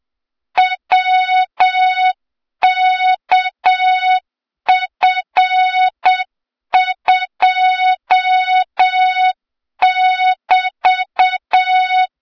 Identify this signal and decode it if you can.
Morse code, heard as a keyed tone, about 8 wpm, 3700 Hz, with noise. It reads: WKF2<BT>
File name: Morze-HA2MN / hang / 23 GYAKORLAT